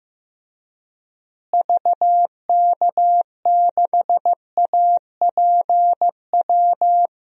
Morse code, heard as a keyed tone, about 15 wpm, 700 Hz. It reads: VK6APW
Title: DXCC-100 Call Signs - 6 Characters - Repeated 3x - 15wpm